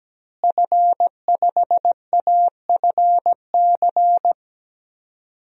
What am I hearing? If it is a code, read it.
Morse code, heard as a keyed tone, about 17 wpm, 700 Hz. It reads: F5AFC
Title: DXCC-100 Call Signs - 5 Characters - Repeated 3x - 17wpm